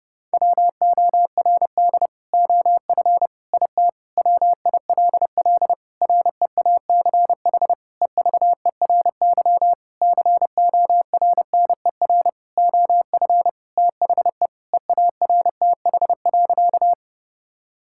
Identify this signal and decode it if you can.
Morse code, heard as a keyed tone, about 30 wpm, 700 Hz. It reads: WORD OF IT WILL REACH EVERY CORNER OF THE EARTH.
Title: Sentences from Top 1600 Words 30wpm